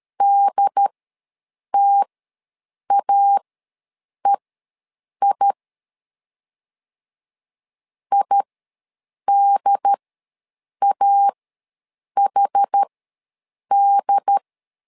Morse code, heard as a keyed tone, about 13 wpm, spaced as for 7 wpm, 800 Hz. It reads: DTAEI IDAHD
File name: LEZIONI / 2 UHDN / 2 UHD ws 12wpm 1722